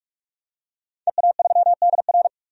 Morse code, heard as a keyed tone, about 46 wpm, 700 Hz. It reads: EA3DR
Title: DXCC-100 Call Signs - 5 Characters - 45wpm